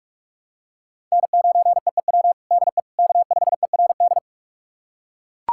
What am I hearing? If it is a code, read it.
Morse code, heard as a keyed tone, about 41 wpm, 700 Hz. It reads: N9EEW DE K5ERD
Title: USA Callsign DE Callsign - 5 Characters - 45wpm